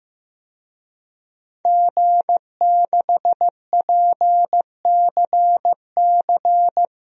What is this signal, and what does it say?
Morse code, heard as a keyed tone, about 15 wpm, 700 Hz. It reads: G6PCC
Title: DXCC-100 Call Signs - 5 Characters - 15wpm